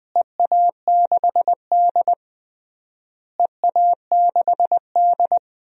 Morse code, heard as a keyed tone, about 20 wpm, 700 Hz. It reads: EA6D EA6D
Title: DXCC-100 Call Signs - 4 Characters - Repeated 3x - 20wpm